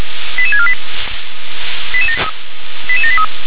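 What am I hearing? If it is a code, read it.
Five-tone selective call (ZVEI), heard as quick runs of five tones, tones of about 70 ms. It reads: 88648 88783
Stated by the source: ZVEI five-tone selective call received on an SDR.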